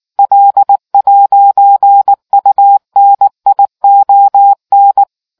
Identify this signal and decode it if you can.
Morse code, heard as a keyed tone, about 19 wpm, 800 Hz. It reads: L'UNION